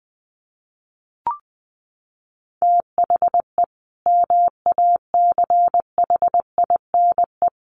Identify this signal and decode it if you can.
Morse code, heard as a keyed tone, about 20 wpm, 700 Hz. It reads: THE MACHINE
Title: A Sound Of Thunder 20wpm